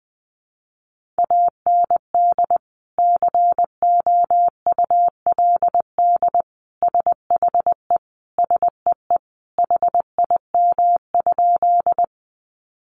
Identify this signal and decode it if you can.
Morse code, heard as a keyed tone, about 20 wpm, 700 Hz. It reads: AND COULD SHE SEE HIM?
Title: Sentences from Top 100 Words 20wpm